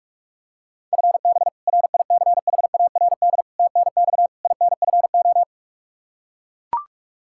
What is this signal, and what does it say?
Morse code, heard as a keyed tone, about 42 wpm, 700 Hz. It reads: FB RICHARD TNX INFO
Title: Practice QSO - Exchange 3 - More Info - 45wpm